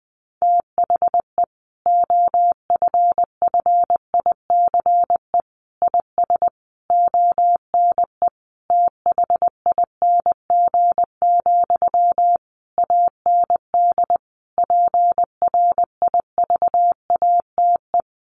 Morse code, heard as a keyed tone, about 20 wpm, 700 Hz. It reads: THE OFFICE IS ONE THING, AND PRIVATE
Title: Sentences from Top 1000 Words 20wpm